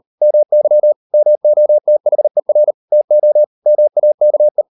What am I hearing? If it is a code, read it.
Morse code, heard as a keyed tone, about 38 wpm, 600 Hz. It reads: MY MOTHER TO MAKE